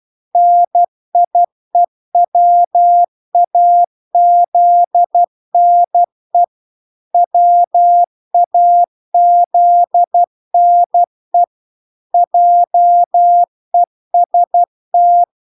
Morse code, heard as a keyed tone, about 12 wpm, 700 Hz. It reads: NIEWAZNE WAZNE JEST